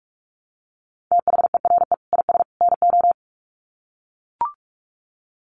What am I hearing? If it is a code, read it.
Morse code, heard as a keyed tone, about 44 wpm, 700 Hz. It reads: THERE IS NO
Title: Sets of 3 Words 45wpm